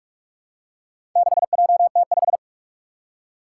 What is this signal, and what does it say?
Morse code, heard as a keyed tone, about 43 wpm, 700 Hz. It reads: 6JT5
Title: Four Challenging Letter-Number Combinations 45wpm